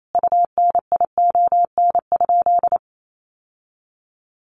Morse code, heard as a keyed tone, about 28 wpm, 700 Hz. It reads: UNION?
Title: Sentences from Top 1400 Words 28wpm